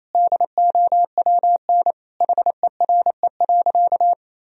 Morse code, heard as a keyed tone, about 28 wpm, 700 Hz. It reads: DOWN HERE.
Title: Sentences from Top 1000 Words - Review Part 3 - 28wpm